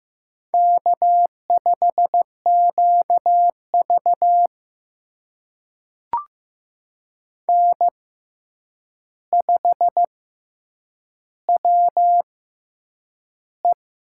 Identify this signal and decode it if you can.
Morse code, heard as a keyed tone, about 15 wpm, spaced as for 8 wpm, 700 Hz. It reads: K5QV N5WE
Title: USA Call Signs - 6x Character Spacing - 15wpm